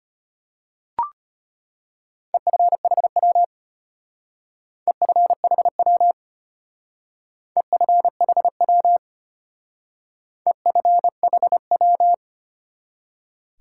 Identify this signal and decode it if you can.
Morse code, heard as a keyed tone, about 32 wpm, 700 Hz. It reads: EFHW EFHW EFHW EFHW